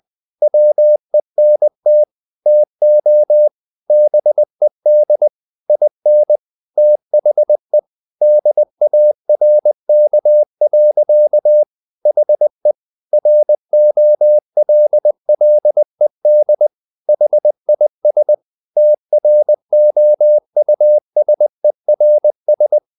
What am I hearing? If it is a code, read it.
Morse code, heard as a keyed tone, about 20 wpm, 600 Hz. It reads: WENT TO BED IN THE DARK. HE ROLLED HIS TROUSERS
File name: the old man and the sea / the old man and the sea part 1 20wpm